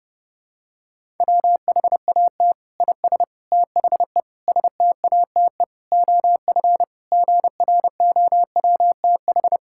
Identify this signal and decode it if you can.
Morse code, heard as a keyed tone, about 32 wpm, 700 Hz. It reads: WHAT IS THE STATE OF GROWTH
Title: Sentences from Top 1000 Words - Review Part 5 - 30wpm